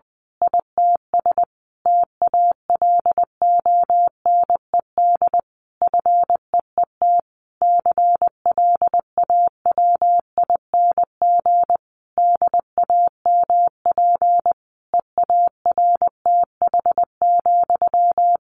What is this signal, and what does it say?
Morse code, heard as a keyed tone, about 20 wpm, 700 Hz. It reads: ITS TALONED FEET CLAWING DAMP EARTH,
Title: A Sound Of Thunder 20wpm